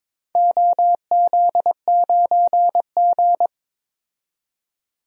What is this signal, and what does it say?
Morse code, heard as a keyed tone, about 22 wpm, 700 Hz. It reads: OZ9G
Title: DXCC-100 Call Signs - 4 Characters - 22wpm